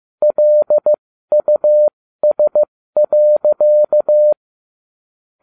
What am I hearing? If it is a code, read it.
Morse code, heard as a keyed tone, about 15 wpm, spaced as for 13 wpm, 600 Hz. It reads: LUS.